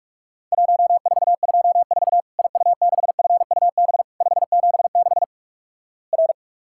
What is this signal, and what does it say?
Morse code, heard as a keyed tone, about 46 wpm, 700 Hz. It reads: 1424 IU6FUB 576 R
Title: Worked All Europe DX Contest - QTC Exchange - 45wpm